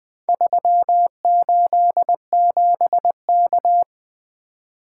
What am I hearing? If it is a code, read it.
Morse code, heard as a keyed tone, about 20 wpm, 700 Hz. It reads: 387K
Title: Four Letter-Number Combinations - 20wpm